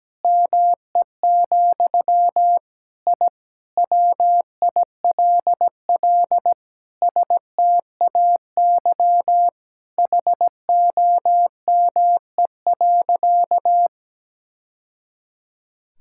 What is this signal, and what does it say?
Morse code, heard as a keyed tone, about 17 wpm, 700 Hz. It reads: ME, I WILL STAY HOME.
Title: Sentences from Top 1000 Words - Review Part 6 - 17wpm